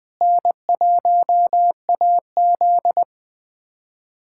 Morse code, heard as a keyed tone, about 20 wpm, 700 Hz. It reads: N1AZ
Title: USA Callsign DE Callsign - 4 Characters - 20wpm